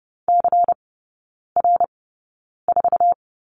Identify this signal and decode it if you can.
Morse code, heard as a keyed tone, about 30 wpm, spaced as for 9 wpm, 700 Hz. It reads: CR4
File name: Lesson 16e - Letter c - Three Characters 30wpm